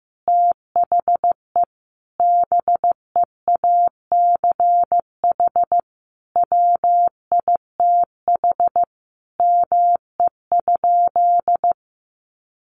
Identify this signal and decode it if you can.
Morse code, heard as a keyed tone, about 15 wpm, 700 Hz. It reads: THE BEACH WITH ME?
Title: Sentences from Top 1500 Words 15wpm